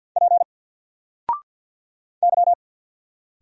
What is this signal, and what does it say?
Morse code, heard as a keyed tone, about 49 wpm, 700 Hz. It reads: P Y
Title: Single Letter-Number Alt 50wpm